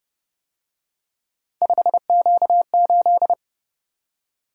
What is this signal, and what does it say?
Morse code, heard as a keyed tone, about 30 wpm, 700 Hz. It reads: 5Q8